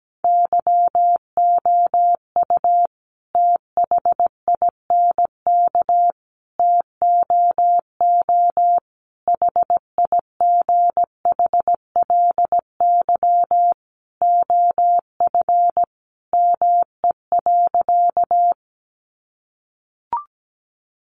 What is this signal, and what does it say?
Morse code, heard as a keyed tone, about 17 wpm, 700 Hz. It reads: YOU THINK TOO HIGHLY OF ME.
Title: Sentences from Top 1100 Words 17wpm